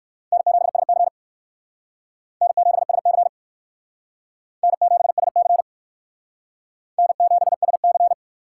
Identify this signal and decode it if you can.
Morse code, heard as a keyed tone, about 58 wpm, 700 Hz. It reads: N7SC N7SC N7SC N7SC